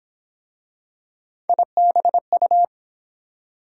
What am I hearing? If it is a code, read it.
Morse code, heard as a keyed tone, about 29 wpm, 700 Hz. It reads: IBU